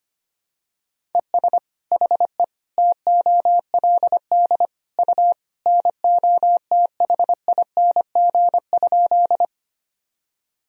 Morse code, heard as a keyed tone, about 25 wpm, 700 Hz. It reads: ES HE TOLD U NOTHING?
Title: Sentences from Top 500 Words and Abbreviations 25wpm